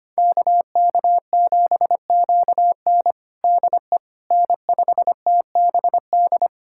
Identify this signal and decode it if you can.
Morse code, heard as a keyed tone, about 25 wpm, 700 Hz. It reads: KK7QN DE N5TBD
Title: USA Callsign DE Callsign - 5 Characters - 25wpm